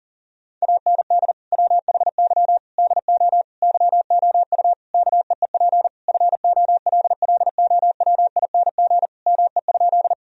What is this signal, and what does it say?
Morse code, heard as a keyed tone, about 40 wpm, 700 Hz. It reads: AND WHY DO YOU KEEP FOLLOWING ME?
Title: Sentences from Top 1000 Words - Review Part 4 - 40wpm